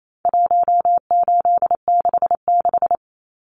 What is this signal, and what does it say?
Morse code, heard as a keyed tone, about 28 wpm, 700 Hz. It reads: 1866
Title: Numbers - 4-Digit 28wpm